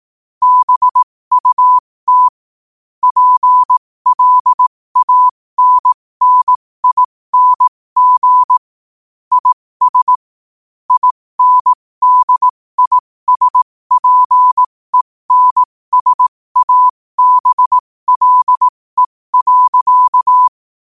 Morse code, heard as a keyed tone, about 18 wpm, spaced as for 15 wpm, 1000 Hz. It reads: BUT PLANNING IS INDISPENSABLE.
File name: qotdfiles / 15wpm